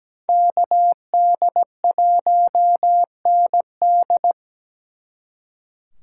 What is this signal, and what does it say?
Morse code, heard as a keyed tone, about 20 wpm, 700 Hz. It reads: KD1ND
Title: Speed Racing - USA 2x2 Call Signs 17wpm